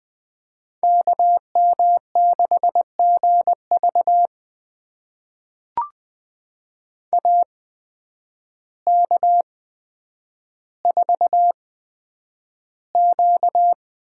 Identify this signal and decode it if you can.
Morse code, heard as a keyed tone, about 21 wpm, spaced as for 9 wpm, 700 Hz. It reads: KM6GV AK4Q
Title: USA Call Signs - 5 Characters - 8x Character Spacing - 20wpm